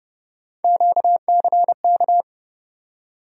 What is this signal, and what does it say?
Morse code, heard as a keyed tone, about 30 wpm, 700 Hz. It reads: QCK
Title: USA Call Sign Postfixes - 3 Letters 30wpm